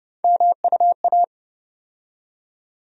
Morse code, heard as a keyed tone, about 30 wpm, 700 Hz. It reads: MUA
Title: USA Call Sign Postfixes - 3 Letters 30wpm